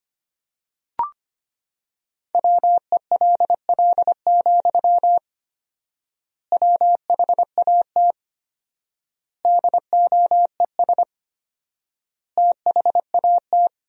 Morse code, heard as a keyed tone, about 25 wpm, 700 Hz. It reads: WELL, WHAT DOES THAT
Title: 4x Word-Spacing - Sentences from Top 500 Words - 25wpm